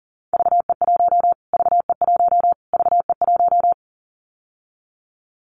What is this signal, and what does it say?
Morse code, heard as a keyed tone, about 40 wpm, 700 Hz. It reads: VE1 VE1 VE1
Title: Callsign Prefixes for Top 25 DXCC - Repeated 3x - 40wpm